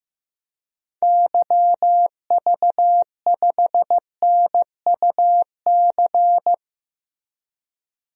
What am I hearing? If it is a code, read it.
Morse code, heard as a keyed tone, about 15 wpm, 700 Hz. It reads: YV5NUC
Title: DXCC-100 Call Signs - 6 Characters - Repeated 2x - 15wpm